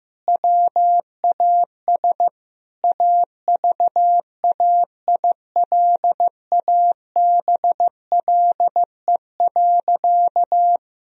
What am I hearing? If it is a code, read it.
Morse code, heard as a keyed tone, about 15 wpm, 700 Hz. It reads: WAS AVAILABLE.
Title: Sentences from Top 1000 Words - Review Part 3 - 15wpm